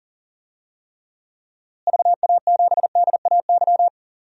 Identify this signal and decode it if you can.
Morse code, heard as a keyed tone, about 40 wpm, 700 Hz. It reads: VA7DAY
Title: DXCC-100 Call Signs - 6 Characters - Repeated 3x - 40wpm